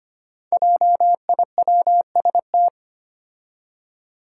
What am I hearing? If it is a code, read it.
Morse code, heard as a keyed tone, about 25 wpm, 700 Hz. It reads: JIWST
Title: Five Challenging Letter-Number Combinations 25wpm